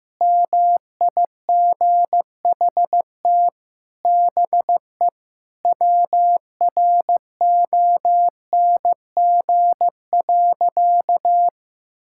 Morse code, heard as a keyed tone, about 15 wpm, 700 Hz. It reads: MIGHT BE WRONG.
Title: Sentences from Top 1600 Words 15wpm